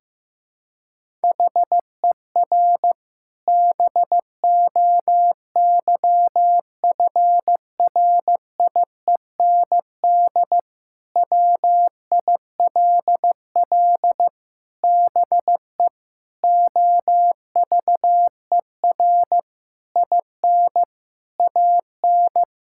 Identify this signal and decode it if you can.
Morse code, heard as a keyed tone, about 15 wpm, 700 Hz. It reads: HER BOYFRIEND WILL BE OVER IN AN
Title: Sentences from Top 1000 Words - Review Part 5 - 15wpm